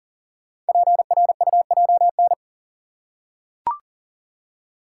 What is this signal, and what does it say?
Morse code, heard as a keyed tone, about 41 wpm, 700 Hz. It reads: PRUJN
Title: Code Groups - 5 Letters - 40wpm